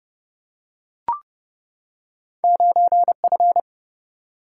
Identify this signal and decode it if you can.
Morse code, heard as a keyed tone, about 30 wpm, 700 Hz. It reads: 9F
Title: Single-Letter Single-Number 30wpm